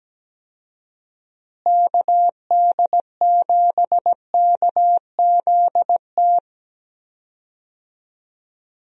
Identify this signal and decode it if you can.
Morse code, heard as a keyed tone, about 17 wpm, 700 Hz. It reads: KD7KZT
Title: USA Call Signs - 6 Characters - Repeated 2x - 17wpm